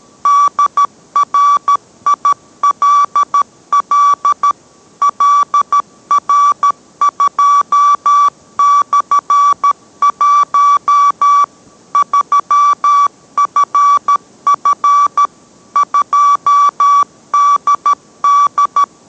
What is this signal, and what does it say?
Morse code, heard as a keyed tone, about 13 wpm, 1150 Hz. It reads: DRILL LR2/1 3FF 2DD